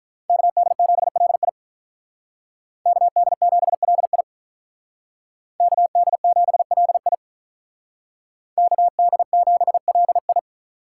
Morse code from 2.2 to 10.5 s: KD7LI KD7LI KD7LI

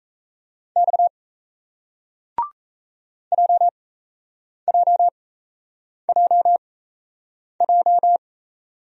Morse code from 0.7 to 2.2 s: X

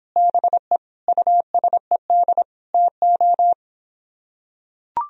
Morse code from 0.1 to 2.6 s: BE USED

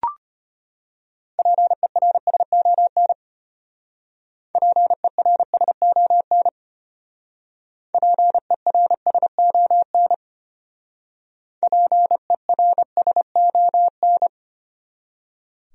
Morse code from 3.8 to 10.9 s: PERSON PERSON

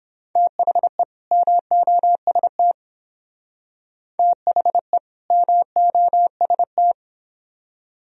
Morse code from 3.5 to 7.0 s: THE MOST

